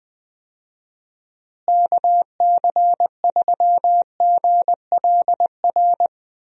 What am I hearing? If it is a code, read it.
Morse code, heard as a keyed tone, about 20 wpm, 700 Hz. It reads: KC3GLR
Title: USA Call Signs - 6 Characters - Repeated 3x - 20wpm